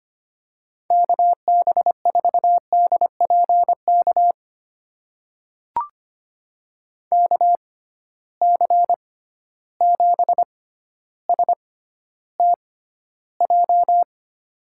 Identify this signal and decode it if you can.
Morse code, heard as a keyed tone, about 25 wpm, spaced as for 13 wpm, 700 Hz. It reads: KB4DPK KC7STJ
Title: USA Call Signs - 6 Characters - 6x Character Spacing - 25wpm